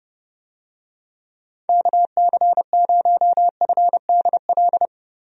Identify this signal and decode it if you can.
Morse code, heard as a keyed tone, about 30 wpm, 700 Hz. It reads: KC0FDL
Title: USA Call Signs - 6 Characters - 30wpm